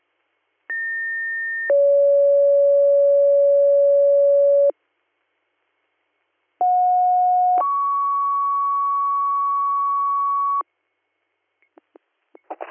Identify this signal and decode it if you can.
Two-tone page, heard as a one-second tone, then a three-second tone, about 1800 then 560 Hz, about 730 then 1120 Hz. It is heard from 0.7 to 10.6 s.